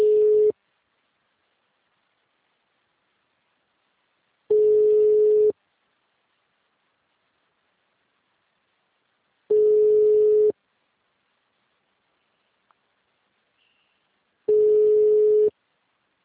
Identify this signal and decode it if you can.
Ringback tone (Europe, 425 Hz).